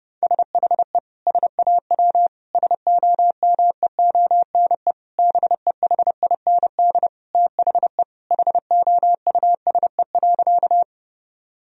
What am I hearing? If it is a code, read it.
Morse code, heard as a keyed tone, about 32 wpm, 700 Hz. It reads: SHE SAW SOMEONE BEHIND THE HOUSE.
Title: Sentences from Top 1000 Words - Review Part 4 - 30wpm